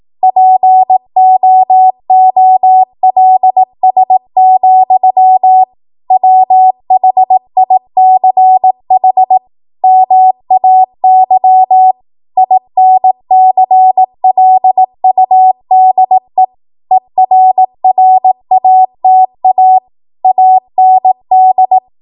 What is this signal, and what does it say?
Morse code, heard as a keyed tone, about 18 wpm, 750 Hz. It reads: POOLS, WHICH MAY INCLUDE ERRATA AND